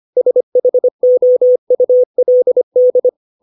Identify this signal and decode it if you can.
Morse code, heard as a keyed tone, about 25 wpm, 500 Hz. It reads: SHOULD